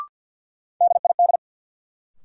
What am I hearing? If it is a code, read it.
Morse code, heard as a keyed tone, about 51 wpm, 700 Hz. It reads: DID
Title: Sentences from Top 100 Words - New Word Drill - 50wpm